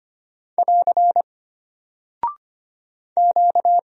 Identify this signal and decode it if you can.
Morse code, heard as a keyed tone, about 25 wpm, 700 Hz. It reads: <AR> Q
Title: Single Character (Letter:Number:Punctuation:Prosign) Alt 25wpm